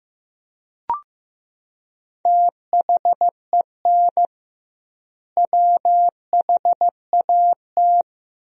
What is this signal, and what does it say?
Morse code, heard as a keyed tone, about 16 wpm, 700 Hz. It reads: THEN WHAT
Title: 2x Word-Spacing - Sentences from Top 100 Words - 15wpm